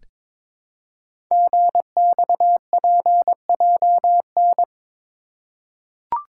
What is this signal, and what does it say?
Morse code, heard as a keyed tone, about 22 wpm, 700 Hz. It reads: GXPJN
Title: Code Groups - 5 Letter:Numbers - 22wpm